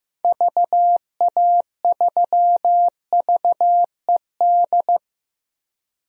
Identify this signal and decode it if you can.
Morse code, heard as a keyed tone, about 15 wpm, 700 Hz. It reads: VA3VED
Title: DXCC-100 Call Signs - 6 Characters - 15wpm